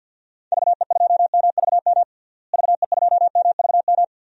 Morse code, heard as a keyed tone, about 50 wpm, 700 Hz. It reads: VE2MVM VE2MVM
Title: DXCC-100 Call Signs - 6 Characters - Repeated 2x - 50wpm